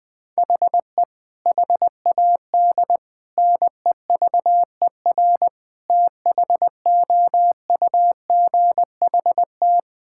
Morse code, heard as a keyed tone, about 20 wpm, 700 Hz. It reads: HE HAD NEVER THOUGHT
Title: Sentences from Top 1600 Words 20wpm